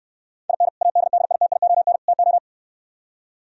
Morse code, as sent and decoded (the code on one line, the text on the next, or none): .. -. / .- -.. -.. .. - .. --- -. / - ---
IN ADDITION TO